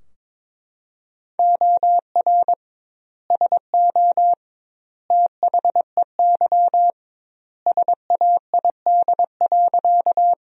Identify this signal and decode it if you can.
Morse code, heard as a keyed tone, about 21 wpm, 700 Hz. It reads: OR SO THEY SAID.
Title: 2x Word-Spacing - Sentences from Top 100 Words - 22wpm